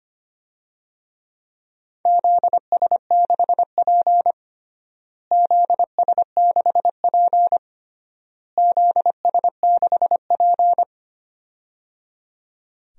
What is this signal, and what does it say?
Morse code, heard as a keyed tone, about 25 wpm, 700 Hz. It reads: ZS6P ZS6P ZS6P
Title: DXCC-100 Call Signs - 4 Characters - Repeated 3x - 25wpm